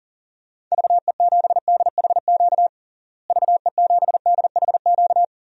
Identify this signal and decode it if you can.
Morse code, heard as a keyed tone, about 37 wpm, 700 Hz. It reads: VE7DHQ VE7DHQ
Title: DXCC-100 Call Signs - 6 Characters - Repeated 3x - 40wpm